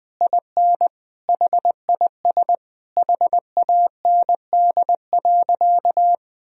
Morse code, heard as a keyed tone, about 20 wpm, 700 Hz. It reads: IN HIS HAND.
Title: Sentences from Top 1000 Words - Review Part 2 - 20wpm